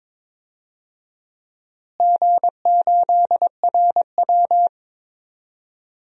Morse code, heard as a keyed tone, about 22 wpm, 700 Hz. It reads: G8RW
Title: DXCC-100 Call Signs - 4 Characters - Repeated 3x - 22wpm